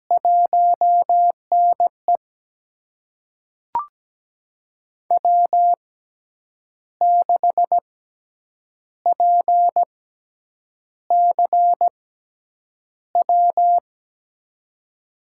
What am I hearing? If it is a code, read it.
Morse code, heard as a keyed tone, about 17 wpm, spaced as for 7 wpm, 700 Hz. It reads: 1NE W6PCW
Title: USA Call Signs - 6x Character Spacing - 17wpm